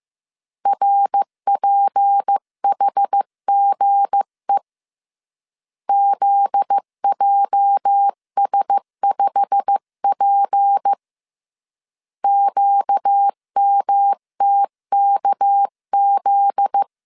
Morse code, heard as a keyed tone, about 15 wpm, 800 Hz. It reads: RPHGE ZJS5P QMTKZ
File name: CW / DET / 1 9 1500C S B 14wpm 3302